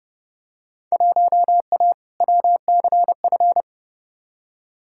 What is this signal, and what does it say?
Morse code, heard as a keyed tone, about 31 wpm, 700 Hz. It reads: 1A WCF
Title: Field Day Exchanges - 30wpm